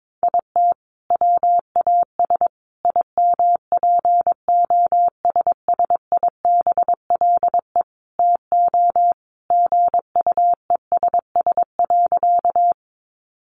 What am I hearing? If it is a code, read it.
Morse code, heard as a keyed tone, about 22 wpm, 700 Hz. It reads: IT WAS IMPOSSIBLE TO GUESS.
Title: Sentences from Top 1500 Words 22wpm